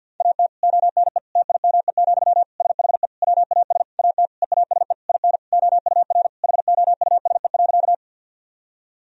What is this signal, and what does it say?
Morse code, heard as a keyed tone, about 53 wpm, 700 Hz. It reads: AT ONE TIME, SHE WAS AT EASE IN OUR HOUSE.
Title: Sentences from Top 1000 Words - Review Part 1 - 50wpm